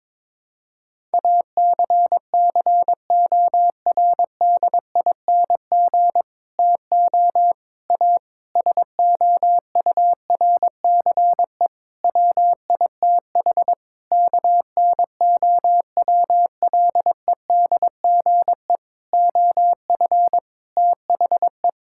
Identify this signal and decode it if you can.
Morse code, heard as a keyed tone, about 22 wpm, 700 Hz. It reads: ACCORDING TO A SOURCE WITH KNOWLEDGE OF THE
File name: Sets of 9 Words - 22wpm